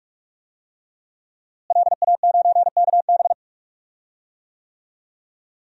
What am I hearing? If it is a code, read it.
Morse code, heard as a keyed tone, about 46 wpm, 700 Hz. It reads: LA9KB
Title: DXCC-100 Call Signs - 5 Characters - Repeated 3x - 45wpm